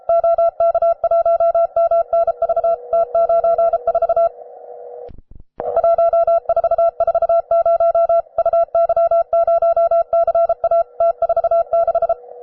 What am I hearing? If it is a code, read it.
Morse code, heard as a keyed tone, about 33 wpm, 670 Hz. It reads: OK1MNV T94 1440 UY0CA T46
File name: wae / cw advanced / rw9sw cw